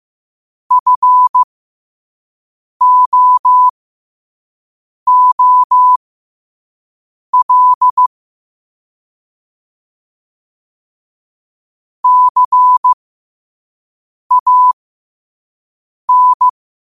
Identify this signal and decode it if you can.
Morse code, heard as a keyed tone, about 15 wpm, spaced as for 5 wpm, 1000 Hz. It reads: FOOL CAN
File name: qotdfiles / 5wpm